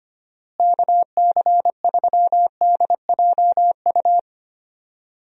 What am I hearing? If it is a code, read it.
Morse code, heard as a keyed tone, about 25 wpm, 700 Hz. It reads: KC3DJU
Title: USA Call Signs - 6 Characters - Repeated 3x - 25wpm